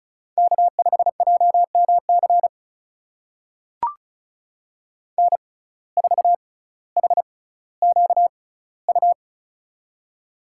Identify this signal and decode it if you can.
Morse code, heard as a keyed tone, about 34 wpm, spaced as for 18 wpm, 700 Hz. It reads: K5JMC N4HQU
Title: USA Call Signs - 5 Characters - 6x Character Spacing - 35wpm